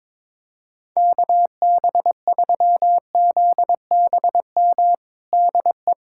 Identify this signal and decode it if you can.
Morse code, heard as a keyed tone, about 22 wpm, 700 Hz. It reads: KB3ZBM DE
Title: USA Callsign DE Callsign - 6 Characters - 22wpm